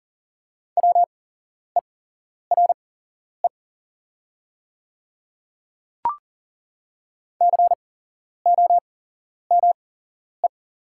Morse code, heard as a keyed tone, about 40 wpm, spaced as for 11 wpm, 690 Hz. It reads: WERE COME
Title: 8x Character Spacing - Top 100 Words - 40wpm